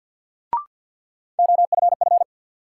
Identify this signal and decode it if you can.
Morse code, heard as a keyed tone, about 53 wpm, 700 Hz. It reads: OFF
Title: Sentences from Top 200 Words - New Word Drill - 50wpm